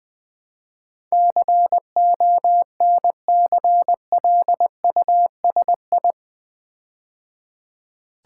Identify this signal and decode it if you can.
Morse code, heard as a keyed tone, about 20 wpm, 700 Hz. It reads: CONCLUSI
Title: Build A Word - 4 Letters At A Time - 10-Letter Words - 20wpm